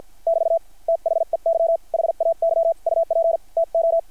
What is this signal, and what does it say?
Morse code, heard as a keyed tone, about 53 wpm, spaced as for 46 wpm, 690 Hz. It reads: <BT> THEY HAY UP TO
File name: archived / CRSnet